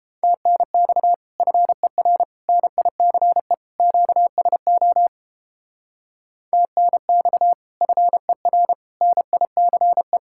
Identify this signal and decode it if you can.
Morse code, heard as a keyed tone, about 29 wpm, 700 Hz. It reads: TNX FER NICE QSO TNX FER NICE